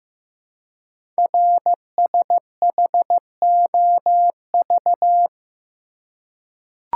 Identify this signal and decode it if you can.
Morse code, heard as a keyed tone, about 15 wpm, 700 Hz. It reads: RSHOV
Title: Code Groups - 5 Letter:Numbers - 15wpm